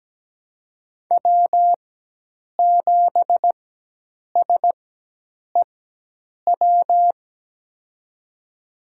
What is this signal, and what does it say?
Morse code, heard as a keyed tone, about 17 wpm, spaced as for 8 wpm, 700 Hz. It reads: W7SEW
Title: USA Call Signs - 5 Characters - 4x Character Spacing - 17wpm